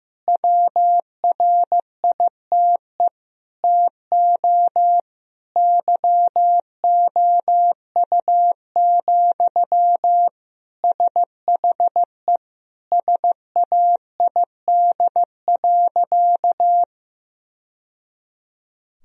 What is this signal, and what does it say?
Morse code, heard as a keyed tone, about 15 wpm, 700 Hz. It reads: WRITE TO YOU, SHE SAID.
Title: Sentences from Top 1000 Words - Review Part 1 - 15wpm